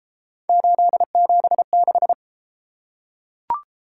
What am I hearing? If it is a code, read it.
Morse code, heard as a keyed tone, about 29 wpm, 700 Hz. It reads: 876